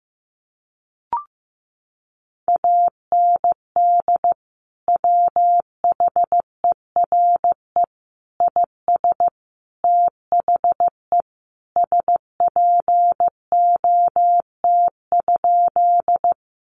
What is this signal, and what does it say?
Morse code, heard as a keyed tone, about 15 wpm, 700 Hz. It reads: AND WHERE IS THE SPOT?
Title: Sentences from Top 1100 Words 15wpm